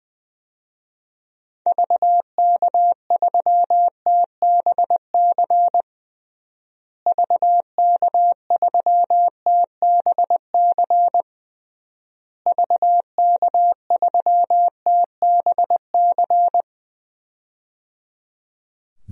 Morse code, heard as a keyed tone, about 20 wpm, 700 Hz. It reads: VK3TBC VK3TBC VK3TBC
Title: DXCC-100 Call Signs - 6 Characters - Repeated 3x - 20wpm